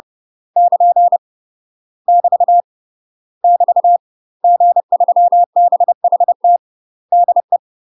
Morse code, heard as a keyed tone, about 30 wpm, 700 Hz. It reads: <KN> <BT> <BT> G3BHT DE
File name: UK MASTER QSOs TRAINING / 30WPM / UK MASTER QSOs 0021